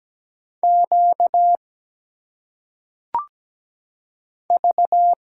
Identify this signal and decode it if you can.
Morse code, heard as a keyed tone, about 17 wpm, 700 Hz. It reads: Q V
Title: Single Letters - Phonetic Pronunciation 17wpm